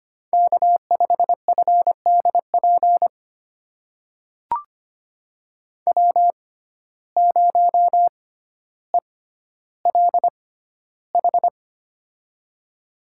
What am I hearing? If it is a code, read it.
Morse code, heard as a keyed tone, about 25 wpm, spaced as for 13 wpm, 700 Hz. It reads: K5FDP W0ELH
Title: USA Call Signs - 6x Character Spacing - 25wpm